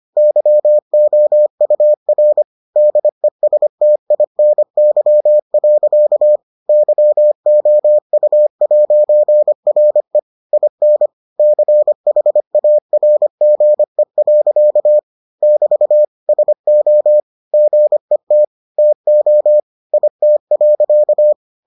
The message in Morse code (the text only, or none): YOUR DESTINY. YOU'RE IN CHARGE. <BT> SO GET TO IT.